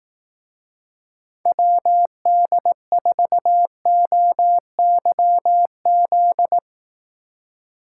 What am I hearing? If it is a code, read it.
Morse code, heard as a keyed tone, about 19 wpm, 700 Hz. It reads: WD4OYZ